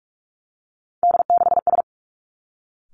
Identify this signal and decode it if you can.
Morse code, heard as a keyed tone, about 46 wpm, 700 Hz. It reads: D6S
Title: Three Challenging Letter-Number Combinations 45wpm